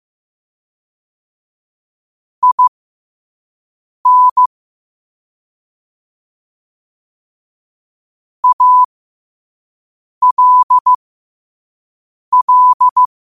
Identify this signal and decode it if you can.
Morse code, heard as a keyed tone, about 15 wpm, spaced as for 5 wpm, 1000 Hz. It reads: IN ALL
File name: qotdfiles / 5wpm